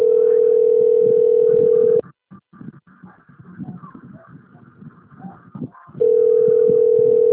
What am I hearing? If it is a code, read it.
Ringback tone (North America), 440 + 480 Hz.
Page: Cornelio Procopio - PR